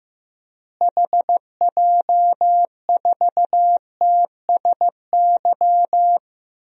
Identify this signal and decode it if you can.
Morse code, heard as a keyed tone, about 15 wpm, 700 Hz. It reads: HJ4TSY
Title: DXCC-100 Call Signs - 6 Characters - Repeated 3x - 15wpm